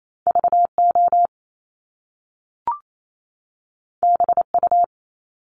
Morse code, heard as a keyed tone, about 28 wpm, 700 Hz. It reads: VO BU
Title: Top Two Letter Combinations in Words 28wpm